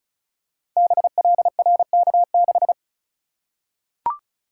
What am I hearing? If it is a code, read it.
Morse code, heard as a keyed tone, about 36 wpm, 690 Hz. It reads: BLRK6
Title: Code Groups - 5 Letter:Numbers - 35wpm